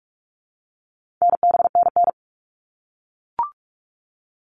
Morse code, heard as a keyed tone, about 44 wpm, 700 Hz. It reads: NBNN
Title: All Cut Numbers - 4-Digit 45wpm